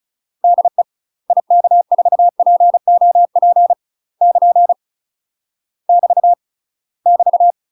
Morse code, heard as a keyed tone, about 36 wpm, 700 Hz. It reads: DE IK4POP <KN> <BT> <BT>
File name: UK MASTER QSOs TRAINING / 35WPM / UK MASTER QSOs 0037